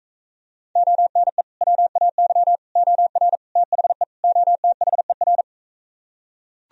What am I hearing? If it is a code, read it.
Morse code, heard as a keyed tone, about 45 wpm, 700 Hz. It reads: ONE WAY OR THE OTHER